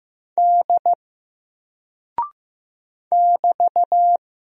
Morse code, heard as a keyed tone, about 15 wpm, 700 Hz. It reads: D <BT>
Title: Single Character (Letter:Number:Punctuation:Prosign) Alt 15wpm